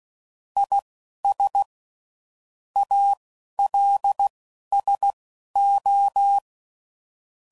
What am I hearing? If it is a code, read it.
Morse code, heard as a keyed tone, about 16 wpm, spaced as for 11 wpm, 800 Hz. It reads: IS ALSO
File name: CW Code Groups @10WPM / Technical Subjects / XL